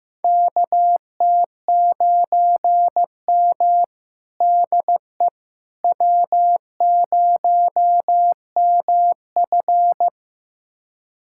Morse code, heard as a keyed tone, about 15 wpm, 700 Hz. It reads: KT9M DE W0MF
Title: USA Callsign DE Callsign - 4 Characters - 15wpm